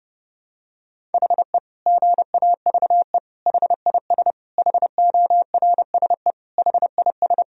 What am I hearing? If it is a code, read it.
Morse code, heard as a keyed tone, about 28 wpm, 700 Hz. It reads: HE GAVE HIS HORSE HIS
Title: Sentences from Top 1000 Words - Review Part 1 - 30wpm